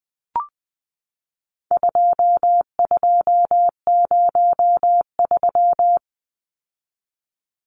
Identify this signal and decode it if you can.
Morse code, heard as a keyed tone, about 20 wpm, 700 Hz. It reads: 2203
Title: Numbers - 4-Digit 20wpm